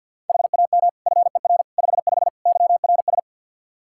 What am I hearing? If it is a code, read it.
Morse code, heard as a keyed tone, about 52 wpm, 700 Hz. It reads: HAM FER 55 YRS
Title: Practice QSO - Exchange 3 - More Info - 50wpm